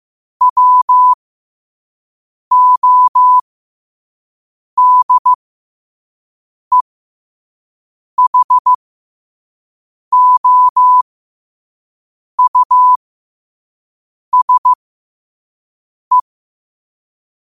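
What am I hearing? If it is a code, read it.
Morse code, heard as a keyed tone, about 15 wpm, spaced as for 5 wpm, 1000 Hz. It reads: WODEHOUSE
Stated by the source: Morse code sent at 5 wpm.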